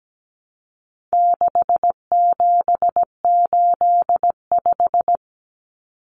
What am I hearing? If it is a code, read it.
Morse code, heard as a keyed tone, about 17 wpm, 700 Hz. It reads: 6785
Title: Common Cut Numbers - 4-Digit 17wpm